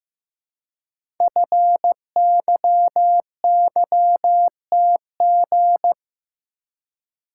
Morse code, heard as a keyed tone, about 15 wpm, 700 Hz. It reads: FYYTG
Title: Code Groups - 5 Letters - 15wpm